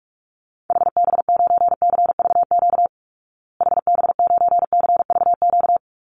Morse code, heard as a keyed tone, about 44 wpm, 700 Hz. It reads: HB9CVQ HB9CVQ
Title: Call Signs Repeated 3x - 45wpm